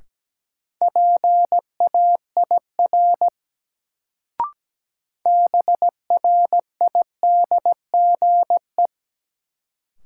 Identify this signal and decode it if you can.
Morse code, heard as a keyed tone, about 17 wpm, 700 Hz. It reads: PAIR BRIDGE
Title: Sentences from Top 1200 Words - New Word Drill Alt - 17wpm